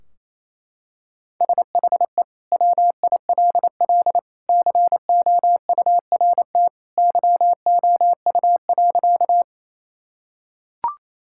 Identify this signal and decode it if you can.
Morse code, heard as a keyed tone, about 27 wpm, 700 Hz. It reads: SHE WILL COURT YOU.
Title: Sentences from Top 1600 Words 28wpm